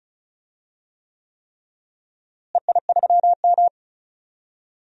Morse code, heard as a keyed tone, about 34 wpm, 700 Hz. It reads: EI3M
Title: DXCC-100 Call Signs - 4 Characters - Repeated 3x - 35wpm